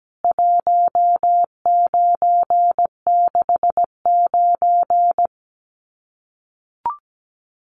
Morse code, heard as a keyed tone, about 17 wpm, 700 Hz. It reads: 1969
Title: CWT Exchange 17wpm